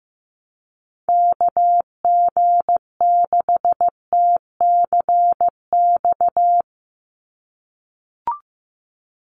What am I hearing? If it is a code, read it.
Morse code, heard as a keyed tone, about 15 wpm, 700 Hz. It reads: KG6TCX
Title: USA Call Signs 15wpm